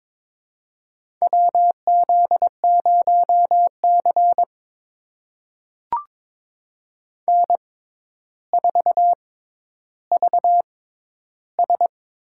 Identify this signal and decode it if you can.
Morse code, heard as a keyed tone, about 21 wpm, spaced as for 11 wpm, 700 Hz. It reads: WZ0C N4VS
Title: USA Call Signs - 4 Characters - 6x Character Spacing - 22wpm